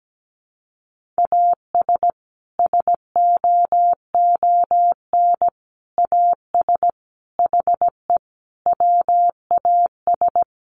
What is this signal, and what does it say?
Morse code, heard as a keyed tone, about 17 wpm, 700 Hz. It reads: AS SOON AS HE WAS
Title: Sentences from Top 1500 Words 17wpm